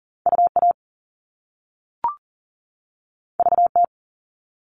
Morse code, heard as a keyed tone, about 41 wpm, 700 Hz. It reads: UA VT
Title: All Cut Numbers - 2-Digit 40wpm